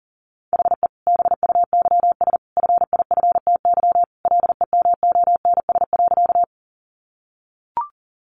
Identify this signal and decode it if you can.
Morse code, heard as a keyed tone, about 40 wpm, 700 Hz. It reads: HE BUYS FIFTY LEMONS.
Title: Additional Easy Phrases 40wpm